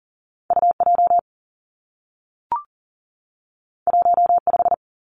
Morse code, heard as a keyed tone, about 42 wpm, 700 Hz. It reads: U J 1 5
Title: Two Challenging Letter-Number Combinations 40wpm